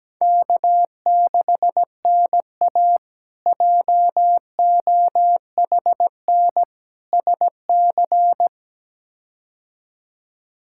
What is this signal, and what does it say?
Morse code, heard as a keyed tone, about 17 wpm, 700 Hz. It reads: K6NA JOHN SC
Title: K1USN SST QSO Practice - 17wpm